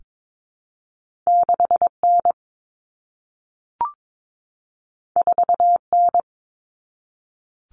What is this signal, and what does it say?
Morse code, heard as a keyed tone, about 22 wpm, 700 Hz. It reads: 6N 4N
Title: Common Cut Numbers - 2-Digit 22wpm